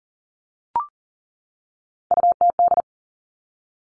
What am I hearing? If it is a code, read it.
Morse code, heard as a keyed tone, about 40 wpm, 700 Hz. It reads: UTD
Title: Three Challenging Letter-Number Combinations 40wpm